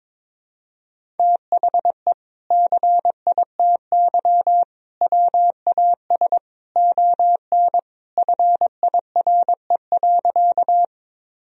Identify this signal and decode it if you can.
Morse code, heard as a keyed tone, about 22 wpm, 700 Hz. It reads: THE CITY WAS ON FIRE.
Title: Sentences from Top 1000 Words - Review Part 1 - 22wpm